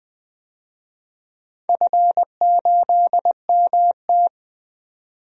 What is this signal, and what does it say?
Morse code, heard as a keyed tone, about 20 wpm, 700 Hz. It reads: F8MT
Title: DXCC-100 Call Signs - 4 Characters - 20wpm